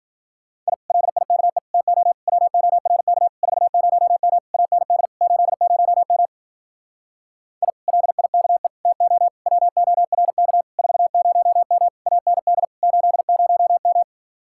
Morse code, heard as a keyed tone, about 50 wpm, 700 Hz. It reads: I LIKE TO WORK 40M AND 80M I LIKE TO WORK 40M AND 80M
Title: Speed-Racing Common QSO Phrases - 40wpm